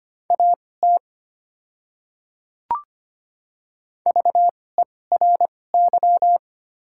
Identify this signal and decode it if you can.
Morse code, heard as a keyed tone, about 25 wpm, spaced as for 18 wpm, 700 Hz. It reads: AT VERY
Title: 2x Character Spacing - Top 200 Words - 25wpm